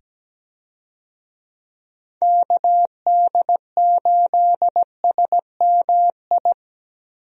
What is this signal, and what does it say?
Morse code, heard as a keyed tone, about 17 wpm, 700 Hz. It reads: KD8SMI
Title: USA Call Signs - 6 Characters - 17wpm